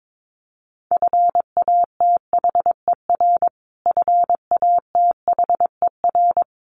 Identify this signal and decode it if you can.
Morse code, heard as a keyed tone, about 22 wpm, 700 Hz. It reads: FATHER FATHER
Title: Top 500 Words Repeated 22wpm